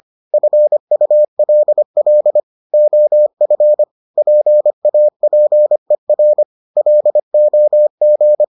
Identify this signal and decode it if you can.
Morse code, heard as a keyed tone, about 25 wpm, 600 Hz. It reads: FULL OF PAPER LOG